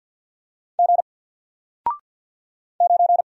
Morse code, heard as a keyed tone, about 50 wpm, 700 Hz. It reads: G 9
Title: Single Character (Letter:Number:Punctuation:Prosign) 50wpm